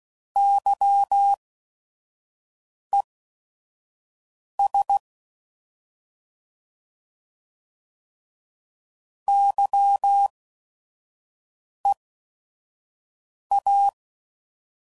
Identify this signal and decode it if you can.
Morse code, heard as a keyed tone, about 16 wpm, spaced as for 5 wpm, 800 Hz. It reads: YES YEA